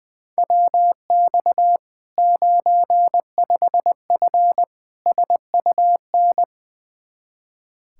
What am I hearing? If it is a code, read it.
Morse code, heard as a keyed tone, about 21 wpm, 700 Hz. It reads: WX 95F SUN
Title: Practice QSO - Exchange 2 - Info - 20wpm